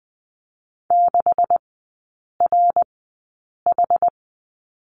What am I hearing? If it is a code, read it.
Morse code, heard as a keyed tone, about 20 wpm, spaced as for 8 wpm, 690 Hz. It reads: 6RH
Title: 172 Lesson 28e - Number 9 - Three Characters 20wpm